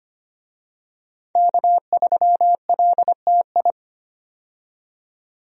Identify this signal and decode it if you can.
Morse code, heard as a keyed tone, about 25 wpm, 700 Hz. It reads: K3LTI
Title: USA Call Signs - 5 Characters - 25wpm